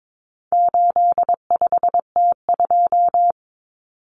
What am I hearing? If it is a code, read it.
Morse code, heard as a keyed tone, about 22 wpm, 700 Hz. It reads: 85T2
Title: Common Cut Numbers - 4-Digit 22wpm